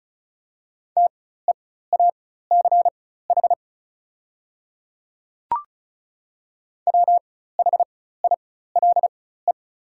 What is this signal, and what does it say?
Morse code, heard as a keyed tone, about 36 wpm, spaced as for 16 wpm, 690 Hz. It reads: TEACH WHILE
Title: 4x Character Spacing - Top 500 Words - 35wpm